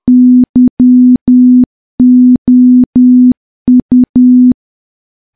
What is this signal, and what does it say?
Morse code, heard as a keyed tone, about 10 wpm, 250 Hz. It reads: YOU
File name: morse / morse-training-10WPM